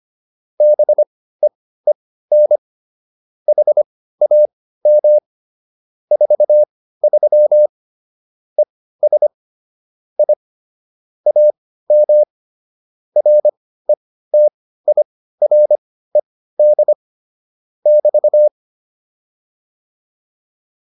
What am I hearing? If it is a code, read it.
Morse code, heard as a keyed tone, about 25 wpm, spaced as for 15 wpm, 600 Hz. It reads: BEEN HAM 43 ES I AM RETIRED <BT>